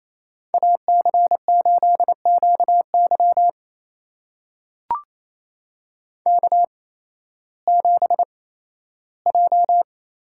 Morse code, heard as a keyed tone, about 28 wpm, spaced as for 15 wpm, 700 Hz. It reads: AC8QY K7J
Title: USA Call Signs - 5 Characters - 8x Character Spacing - 28wpm